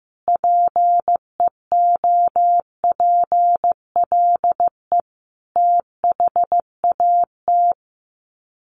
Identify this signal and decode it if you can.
Morse code, heard as a keyed tone, about 15 wpm, 700 Hz. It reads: PEOPLE THAT
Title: Sets of 5 Words Encore2 15wpm